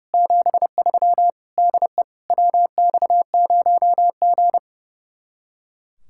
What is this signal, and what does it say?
Morse code, heard as a keyed tone, about 35 wpm, 700 Hz. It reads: 73 DE WX0G
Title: Speed-Racing Common QSO Phrases Encore - 30wpm